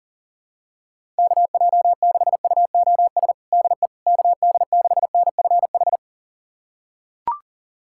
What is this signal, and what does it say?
Morse code, heard as a keyed tone, about 40 wpm, 700 Hz. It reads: KJ6UOS DE KD6NFH
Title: USA Callsign DE Callsign - 6 Characters - 40wpm